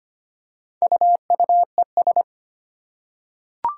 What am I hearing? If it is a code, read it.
Morse code, heard as a keyed tone, about 25 wpm, 700 Hz. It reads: UUES
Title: Four Challenging Letter-Number Combinations 25wpm